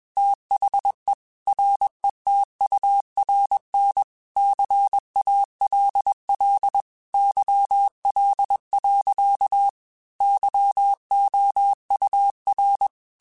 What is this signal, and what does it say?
Morse code, heard as a keyed tone, about 21 wpm, 800 Hz. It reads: THE RETURN CALL YL. YOUR
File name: QSOs / QSOs-mp3s / WA7QOX 2@20